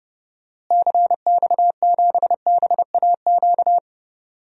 Morse code, heard as a keyed tone, about 30 wpm, 700 Hz. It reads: CX7BAQ
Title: DXCC-100 Call Signs - 6 Characters - Repeated 2x - 30wpm